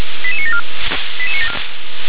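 Five-tone selective call (ZVEI), heard as quick runs of five tones, tones of about 70 ms. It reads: 88974
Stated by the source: ZVEI five-tone call received by SDR.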